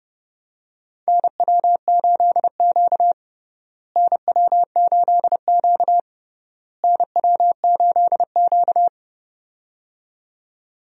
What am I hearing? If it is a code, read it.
Morse code, heard as a keyed tone, about 31 wpm, 700 Hz. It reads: NW8Q NW8Q NW8Q
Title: USA Call Signs - 4 Characters - Repeated 3x - 30wpm